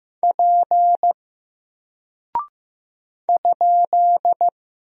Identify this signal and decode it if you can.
Morse code, heard as a keyed tone, about 15 wpm, 700 Hz. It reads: P ?
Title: Single Character (Letter:Number:Punctuation:Prosign) 15wpm